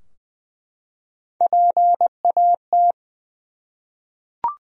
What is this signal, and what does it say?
Morse code, heard as a keyed tone, about 20 wpm, 700 Hz. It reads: PAT
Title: USA Common Names - Max 3 Letters - 20wpm